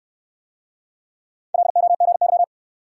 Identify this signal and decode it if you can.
Morse code, heard as a keyed tone, about 66 wpm, 700 Hz. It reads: LOGY